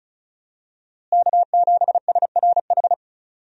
Morse code, heard as a keyed tone, about 36 wpm, 700 Hz. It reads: K7SRH
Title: USA Call Signs - 5 Characters - Repeated 3x - 35wpm